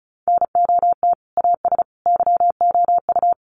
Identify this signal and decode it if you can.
Morse code, heard as a keyed tone, about 34 wpm, 700 Hz. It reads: NOT AS YOU
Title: Sentences from Top 1200 Words 35wpm